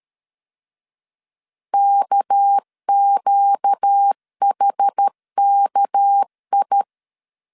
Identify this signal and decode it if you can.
Morse code, heard as a keyed tone, about 13 wpm, 800 Hz. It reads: KQHKI
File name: LEZIONI / 6 PKX0 / 6 PK 12wpm 2006